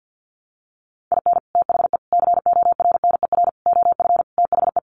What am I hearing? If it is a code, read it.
Morse code, heard as a keyed tone, about 50 wpm, 700 Hz. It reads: IN THE CORNER OF THE
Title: Sets of 5 Words Encore 50wpm